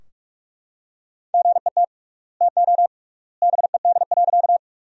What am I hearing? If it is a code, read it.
Morse code, heard as a keyed tone, about 41 wpm, 700 Hz. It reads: GET TO BED.
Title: 3x Word-Spacing - Sentences from Top 500 Words - 45wpm